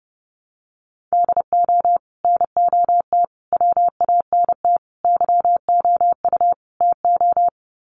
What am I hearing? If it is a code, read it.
Morse code, heard as a keyed tone, about 30 wpm, 700 Hz. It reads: DO NOT WANT YOU TO
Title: Sets of 5 Words 30wpm